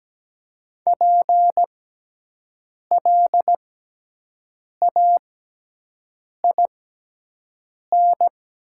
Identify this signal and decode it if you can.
Morse code, heard as a keyed tone, about 17 wpm, spaced as for 6 wpm, 700 Hz. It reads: PLAIN
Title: 6x Character Spacing - Top 500 Words - 17wpm